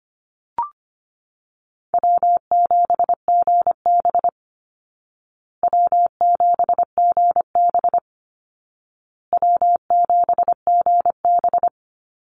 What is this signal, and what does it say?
Morse code, heard as a keyed tone, about 25 wpm, 700 Hz. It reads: W7GB W7GB W7GB
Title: Easy Call Signs Repeated 3x - 25wpm